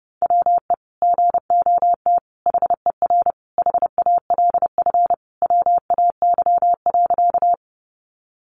Morse code, heard as a keyed tone, about 30 wpm, 700 Hz. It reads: WE GOT HER HALF WAY.
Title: Sentences from Top 400 Words 30wpm